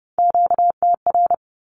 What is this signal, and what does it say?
Morse code, heard as a keyed tone, about 30 wpm, 700 Hz. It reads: QTR
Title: Q-Codes with Spoken Meaning - 30wpm